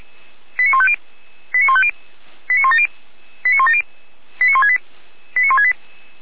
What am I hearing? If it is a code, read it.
Five-tone selective call (ZVEI), heard as quick runs of five tones, tones of about 80 ms. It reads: 89150 89150 89170 89170 89168 89168
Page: /455/SDR/ZVEI/